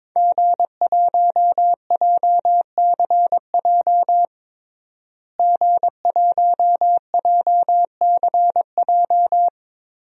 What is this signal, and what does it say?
Morse code, heard as a keyed tone, about 22 wpm, 700 Hz. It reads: G1JCJ G1JCJ
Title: DXCC-100 Call Signs - 5 Characters - Repeated 3x - 22wpm